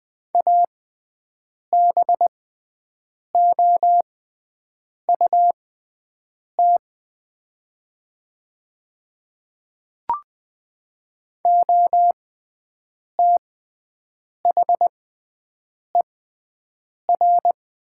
Morse code, heard as a keyed tone, about 20 wpm, spaced as for 7 wpm, 690 Hz. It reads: ABOUT OTHER
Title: 6x Character Spacing - Top 300 Words - 20wpm